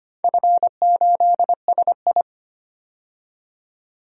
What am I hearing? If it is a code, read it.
Morse code, heard as a keyed tone, about 25 wpm, 700 Hz. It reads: F8SI
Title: DXCC-100 Call Signs - 4 Characters - 25wpm